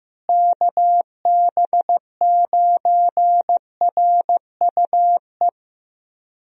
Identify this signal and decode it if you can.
Morse code, heard as a keyed tone, about 15 wpm, 700 Hz. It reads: KB9RUE
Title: USA Call Signs - 6 Characters - Repeated 3x - 15wpm